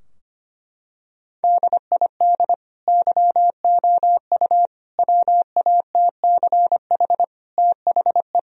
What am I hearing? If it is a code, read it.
Morse code, heard as a keyed tone, about 25 wpm, 700 Hz. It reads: DID YOU WATCH THE
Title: Sentences from Top 1000 Words - Review Part 4 - 25wpm